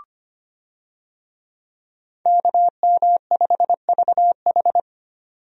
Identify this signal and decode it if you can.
Morse code, heard as a keyed tone, about 25 wpm, 700 Hz. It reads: KM5VH
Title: USA Call Signs - 5 Characters - 25wpm